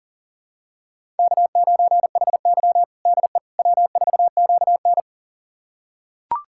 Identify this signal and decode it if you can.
Morse code, heard as a keyed tone, about 40 wpm, 700 Hz. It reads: K9HY DE W4QN
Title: USA Callsign DE Callsign - 4 Characters - 40wpm